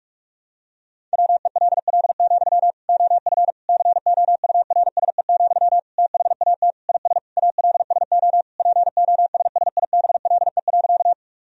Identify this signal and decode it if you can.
Morse code, heard as a keyed tone, about 46 wpm, 700 Hz. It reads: WELL, OF COURSE, THAT IS ALSO POSSIBLE.
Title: Sentences from Top 1000 Words - Review Part 1 - 45wpm